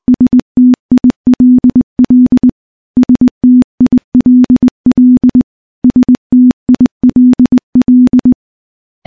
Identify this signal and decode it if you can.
Morse code, heard as a keyed tone, about 20 wpm, 250 Hz. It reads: STILL STILL STILL
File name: morse / morse-training-20WPM